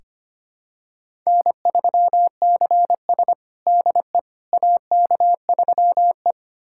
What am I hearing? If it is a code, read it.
Morse code, heard as a keyed tone, about 25 wpm, 700 Hz. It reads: N3CS DE AK3E
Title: USA Callsign DE Callsign - 4 Characters - 25wpm